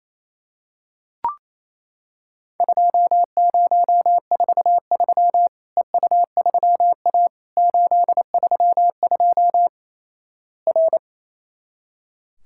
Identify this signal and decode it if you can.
Morse code, heard as a keyed tone, about 28 wpm, 700 Hz. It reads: 2043 EU3A 832 R
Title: Worked All Europe DX Contest - QTC Exchange - 28wpm